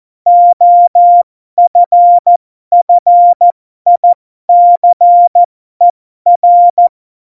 Morse code, heard as a keyed tone, about 14 wpm, spaced as for 12 wpm, 700 Hz. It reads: OFFICER
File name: Morsekurs / Witze